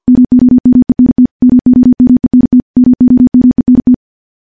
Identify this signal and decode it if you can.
Morse code, heard as a keyed tone, about 50 wpm, 250 Hz. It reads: MOMENT MOMENT MOMENT
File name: morse-training-50WPM